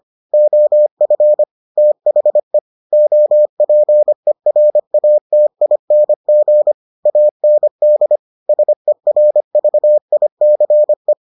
Morse code, heard as a keyed tone, about 25 wpm, 600 Hz. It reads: OF THE OPERATING AND SERVICE